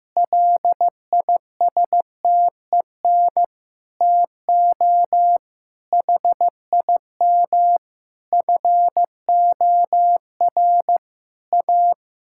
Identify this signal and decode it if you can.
Morse code, heard as a keyed tone, about 15 wpm, 700 Hz. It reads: LISTEN TO HIM FOR A
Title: Sentences from Top 1600 Words 15wpm